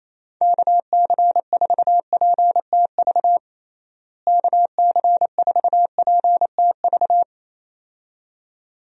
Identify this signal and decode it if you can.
Morse code, heard as a keyed tone, about 28 wpm, 700 Hz. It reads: KC4PTV KC4PTV
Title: USA Call Signs - 6 Characters - Repeated 2x - 28wpm